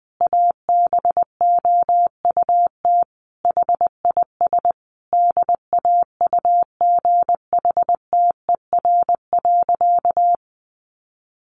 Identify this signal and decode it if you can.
Morse code, heard as a keyed tone, about 20 wpm, 700 Hz. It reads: ABOUT HIS DAUGHTER.
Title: Sentences from Top 1500 Words 20wpm